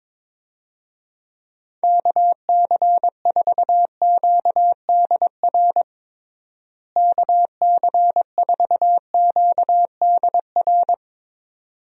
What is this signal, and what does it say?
Morse code, heard as a keyed tone, about 22 wpm, 700 Hz. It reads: KC4QDR KC4QDR
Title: USA Call Signs - 6 Characters - Repeated 2x - 22wpm